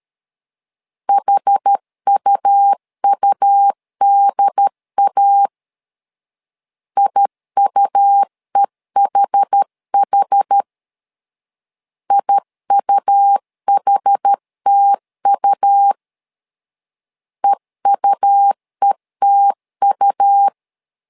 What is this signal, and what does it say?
Morse code, heard as a keyed tone, about 13 wpm, 800 Hz. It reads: HUUDA IUEHH IUHTU EUETU
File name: LEZIONI / 2 UHDN / 2 UHD 12wpm 1540